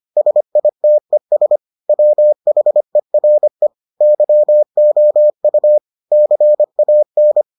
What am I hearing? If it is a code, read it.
Morse code, heard as a keyed tone, about 25 wpm, 600 Hz. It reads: SITES WHERE YOU CAN